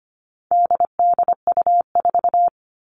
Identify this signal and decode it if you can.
Morse code, heard as a keyed tone, about 25 wpm, 700 Hz. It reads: DDU4
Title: All Cut Numbers - 4-Digit 25wpm